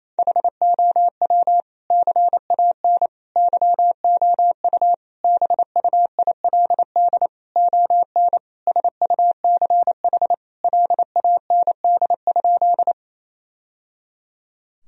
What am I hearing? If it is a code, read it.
Morse code, heard as a keyed tone, about 27 wpm, 700 Hz. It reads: HOW CAN YOU BUILD ON SUCH LAND?
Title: Sentences from Top 1000 Words - Review Part 1 - 28wpm